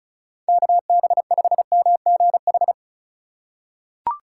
Morse code, heard as a keyed tone, about 34 wpm, 700 Hz. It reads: KB5MGH
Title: USA Call Signs - 4x Character Spacing - 35wpm